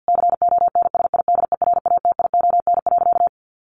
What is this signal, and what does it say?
Morse code, heard as a keyed tone, about 50 wpm, 700 Hz. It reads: CONSIDERATION.
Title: Sentences from Top 1500 Words 50wpm